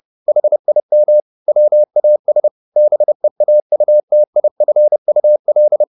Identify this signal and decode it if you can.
Morse code, heard as a keyed tone, about 30 wpm, 600 Hz. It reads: HIM WAS BEAUTIFUL